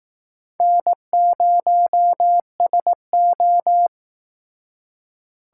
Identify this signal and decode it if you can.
Morse code, heard as a keyed tone, about 18 wpm, 700 Hz. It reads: N0SO
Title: Speed Racing - USA 1x2 Call Signs 15wpm